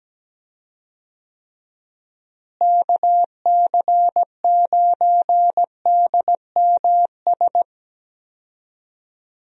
Morse code, heard as a keyed tone, about 17 wpm, 700 Hz. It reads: KC9DMS
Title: USA Call Signs - 6 Characters - Repeated 3x - 17wpm